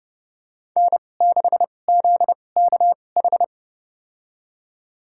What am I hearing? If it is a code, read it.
Morse code, heard as a keyed tone, about 30 wpm, spaced as for 22 wpm, 700 Hz. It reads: N6ZKH
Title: USA Call Signs - 5 Characters - 2x Character Spacing - 30wpm